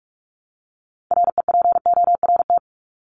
Morse code, heard as a keyed tone, about 44 wpm, 700 Hz. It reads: REPORT
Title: ED Suffix - Top 100 Words with Base Words 45wpm